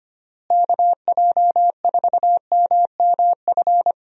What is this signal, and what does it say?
Morse code, heard as a keyed tone, about 25 wpm, 700 Hz. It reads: KJ4MMF
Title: USA Call Signs - 6 Characters - Repeated 3x - 25wpm